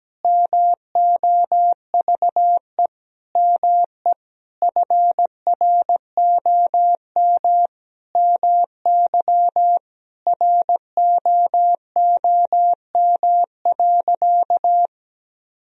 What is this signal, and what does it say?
Morse code, heard as a keyed tone, about 17 wpm, 700 Hz. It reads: MOVE ME FROM MY ROOM.
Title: Sentences from Top 1000 Words - Review Part 1 - 17wpm